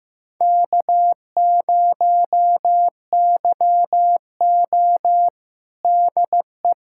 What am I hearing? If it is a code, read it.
Morse code, heard as a keyed tone, about 16 wpm, 700 Hz. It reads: K0YO DE
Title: USA Callsign DE Callsign - 4 Characters - 15wpm